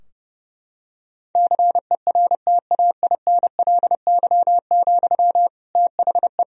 Morse code, heard as a keyed tone, about 32 wpm, 700 Hz. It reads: CERTAINLY, THE
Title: Sentences from Top 1000 Words - Review Part 3 - 30wpm